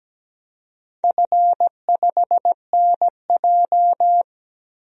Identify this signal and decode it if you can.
Morse code, heard as a keyed tone, about 17 wpm, 700 Hz. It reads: F5NJ
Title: DXCC-100 Call Signs - 4 Characters - 17wpm